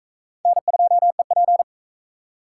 Morse code, heard as a keyed tone, about 41 wpm, 700 Hz. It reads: N2EP